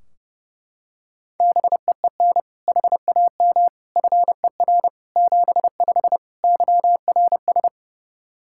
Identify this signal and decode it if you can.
Morse code, heard as a keyed tone, about 30 wpm, 700 Hz. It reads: BEEN HAM FER 75 YRS
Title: Practice QSO - Exchange 3 - More Info - 30wpm